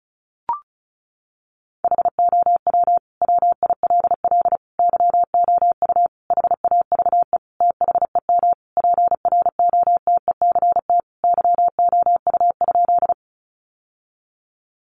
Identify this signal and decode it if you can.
Morse code, heard as a keyed tone, about 34 wpm, 700 Hz. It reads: HOW WILL YOU HAVE THEM PROTECT YOU?
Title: Sentences from Top 900 Words 35wpm